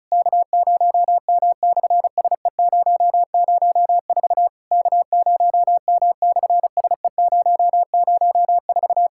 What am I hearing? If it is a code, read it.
Morse code, heard as a keyed tone, about 34 wpm, 690 Hz. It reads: K0M/SE004 K0M/SE004
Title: SOTA QSO Practice 35wpm